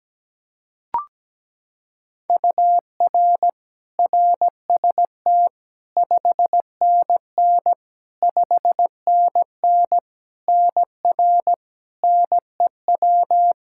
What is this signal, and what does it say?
Morse code, heard as a keyed tone, about 17 wpm, 700 Hz. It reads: UR RST 5NN 5NN NR NEW
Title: Practice QSO - Exchange 1 - Beginning - 17wpm